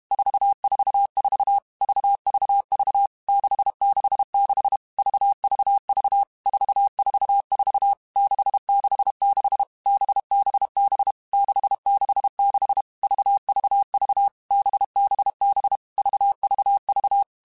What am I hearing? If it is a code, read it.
Morse code, heard as a keyed tone, about 32 wpm, 800 Hz. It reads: 444 VVV 666 VVV 444 666 BBB 666 VVV BBB VVV
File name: Groups@30 / Groups@30 mp3s / VB46@30